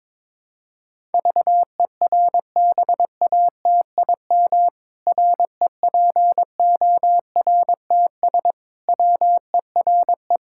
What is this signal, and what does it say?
Morse code, heard as a keyed tone, about 22 wpm, 700 Hz. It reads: VERBATIM REPORTS WERE
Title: Pangram Challenge v2 - 22wpm